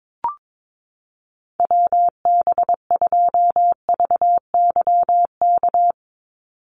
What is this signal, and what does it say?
Morse code, heard as a keyed tone, about 22 wpm, 700 Hz. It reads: WB2VYK
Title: Call Signs - 22wpm